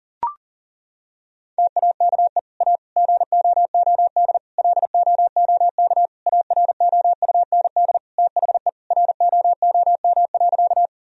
Morse code, heard as a keyed tone, about 40 wpm, 700 Hz. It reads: TAKE A GOOD LOOK AROUND THE ROOM.
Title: Sentences from Top 1000 Words - Review Part 3 - 40wpm